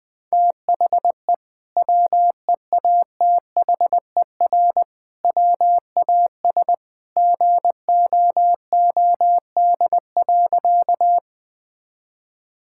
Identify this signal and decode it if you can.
Morse code, heard as a keyed tone, about 20 wpm, 700 Hz. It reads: THE WEATHER WAS GOOD.
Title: Sentences from Top 1000 Words - Review Part 5 - 20wpm